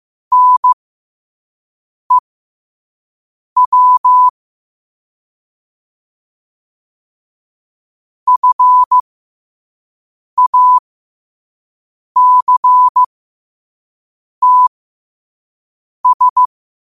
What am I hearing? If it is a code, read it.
Morse code, heard as a keyed tone, about 15 wpm, spaced as for 5 wpm, 1000 Hz. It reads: NEW FACTS
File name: qotdfiles / 5wpm